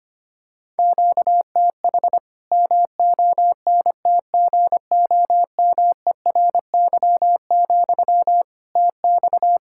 Morse code, heard as a keyed tone, about 25 wpm, 700 Hz. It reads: QTH MONTGOMERY, TX
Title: K1USN SST QSO Practice 25wpm